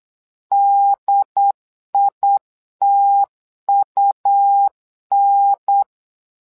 Morse code, heard as a keyed tone, about 8 wpm, 800 Hz. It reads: DITUN
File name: LEZIONI / 2 UHDN / 2 UHDN 8wpm2232